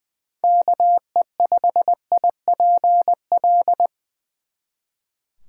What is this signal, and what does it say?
Morse code, heard as a keyed tone, about 24 wpm, 700 Hz. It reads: KE5IPL
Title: Speed Racing - USA 2x3 Call Signs 20wpm